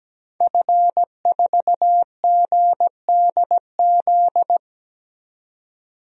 Morse code, heard as a keyed tone, about 17 wpm, 700 Hz. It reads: F4GDZ
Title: DXCC-100 Call Signs - 5 Characters - 17wpm